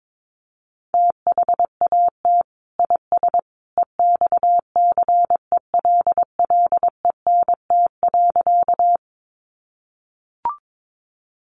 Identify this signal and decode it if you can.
Morse code, heard as a keyed tone, about 22 wpm, 700 Hz. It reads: THAT IS EXCELLENT.
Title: Sentences from Top 1000 Words 22wpm